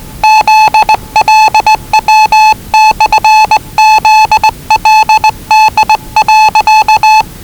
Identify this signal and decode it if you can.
Morse code, heard as a keyed tone, about 20 wpm, 840 Hz. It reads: ZLW/ZLD.
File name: WellingtonRadioZlwCloseDownOnHfAndMf / 01-Zlw 4277khz 417.5 qrt until 0925z